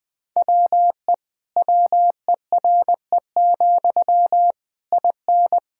Morse code, heard as a keyed tone, about 20 wpm, 700 Hz. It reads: WE WERE, IN
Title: Sentences from Top 1000 Words - Review Part 5 - 20wpm